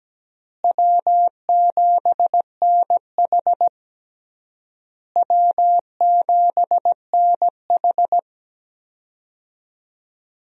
Morse code, heard as a keyed tone, about 17 wpm, 700 Hz. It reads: W7NH W7NH
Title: USA Call Signs - 4 Characters - Repeated 3x - 17wpm